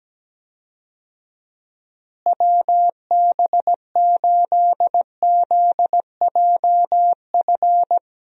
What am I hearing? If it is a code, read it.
Morse code, heard as a keyed tone, about 17 wpm, 700 Hz. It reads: WB8ZJF
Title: USA Call Signs - 6 Characters - Repeated 3x - 17wpm